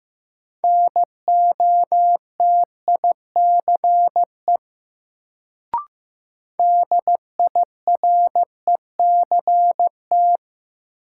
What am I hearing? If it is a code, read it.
Morse code, heard as a keyed tone, about 15 wpm, 700 Hz. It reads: NOTICE DIRECT
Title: Sentences from Top 400 Words - New Word Drill - 15wpm